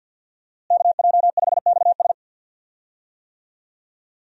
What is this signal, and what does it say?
Morse code, heard as a keyed tone, about 47 wpm, 700 Hz. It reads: KJ5XS
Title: USA Call Signs - 5 Characters - Repeated 2x - 50wpm